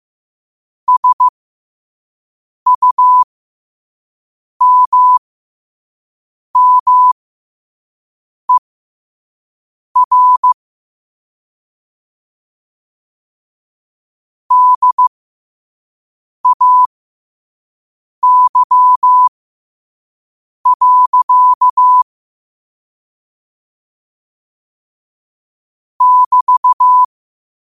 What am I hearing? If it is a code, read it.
Morse code, heard as a keyed tone, about 15 wpm, spaced as for 5 wpm, 1000 Hz. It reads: SUMMER DAY. <BT>
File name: qotdfiles / 5wpm